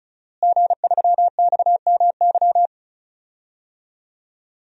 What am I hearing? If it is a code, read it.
Morse code, heard as a keyed tone, about 34 wpm, 700 Hz. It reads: G3XMY
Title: DXCC-100 Call Signs - 5 Characters - Repeated 2x - 35wpm